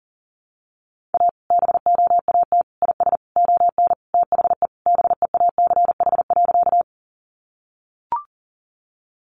Morse code, heard as a keyed tone, about 37 wpm, 700 Hz. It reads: A BOAT IS ON THE BEACH.
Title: Sentences from Top 1500 Words 40wpm